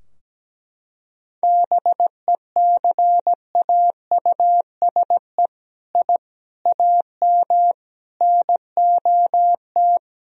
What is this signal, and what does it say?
Morse code, heard as a keyed tone, about 17 wpm, 700 Hz. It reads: BECAUSE I AM NOT
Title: Sentences from Top 1000 Words - Review Part 3 - 17wpm